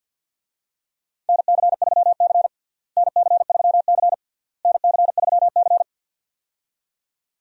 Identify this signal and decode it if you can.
Morse code, heard as a keyed tone, about 52 wpm, 700 Hz. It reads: NC3C NC3C NC3C
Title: USA Call Signs - 4 Characters - Repeated 3x - 50wpm